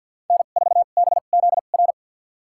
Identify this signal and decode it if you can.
Morse code, heard as a keyed tone, about 51 wpm, spaced as for 36 wpm, 700 Hz. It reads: N4BZR
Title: USA Call Signs - 5 Characters - 2x Character Spacing - 50wpm